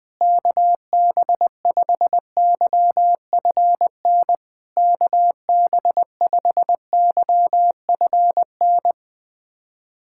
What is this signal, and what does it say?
Morse code, heard as a keyed tone, about 20 wpm, 700 Hz. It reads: KB5YFN KB5YFN
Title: SOTA QSO Practice 20wpm